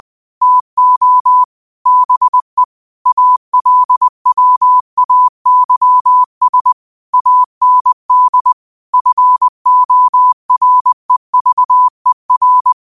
Morse code, heard as a keyed tone, about 20 wpm, 1000 Hz. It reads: TO BE ALWAYS AND FOREVER
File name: qotdfiles / 20wpm